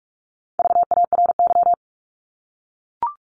Morse code, heard as a keyed tone, about 44 wpm, 700 Hz. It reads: VARY
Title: Top 1001-1250 Words 45wpm